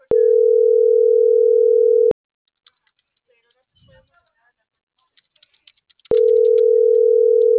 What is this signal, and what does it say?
Ringback tone (North America), 440 + 480 Hz.